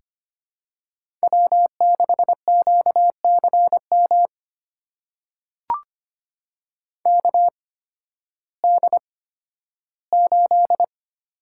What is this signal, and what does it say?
Morse code, heard as a keyed tone, about 25 wpm, spaced as for 13 wpm, 700 Hz. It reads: W6QCM KD8
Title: USA Call Signs - 5 Characters - 8x Character Spacing - 25wpm